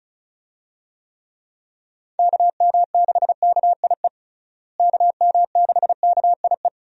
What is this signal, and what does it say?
Morse code, heard as a keyed tone, about 34 wpm, 700 Hz. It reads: KM6KIE KM6KIE
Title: USA Call Signs - 6 Characters - Repeated 3x - 35wpm